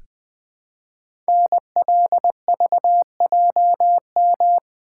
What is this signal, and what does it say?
Morse code, heard as a keyed tone, about 20 wpm, 700 Hz. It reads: NLVJM
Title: Code Groups - 5 Letters - 20wpm